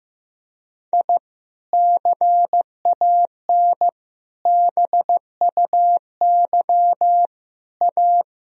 Morse code, heard as a keyed tone, about 15 wpm, 700 Hz. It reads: I CAN BUY A
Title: Sentences from Top 1000 Words - Review Part 1 - 15wpm